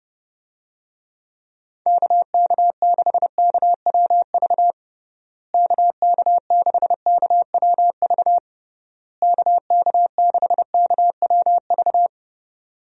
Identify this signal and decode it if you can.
Morse code, heard as a keyed tone, about 32 wpm, 700 Hz. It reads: KK6KWV KK6KWV KK6KWV
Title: USA Call Signs - 6 Characters - Repeated 3x - 30wpm